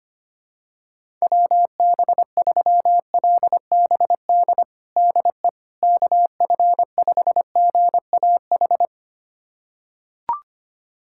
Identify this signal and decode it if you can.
Morse code, heard as a keyed tone, about 25 wpm, 700 Hz. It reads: WB3LBD DE KF5GAH
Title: USA Callsign DE Callsign 25wpm